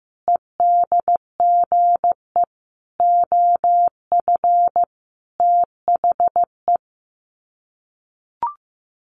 Morse code, heard as a keyed tone, about 15 wpm, 700 Hz. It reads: EDGE OF THE
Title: Sets of 5 Words Encore2 15wpm